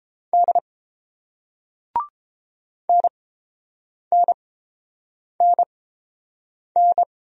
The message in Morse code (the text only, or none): D NNNN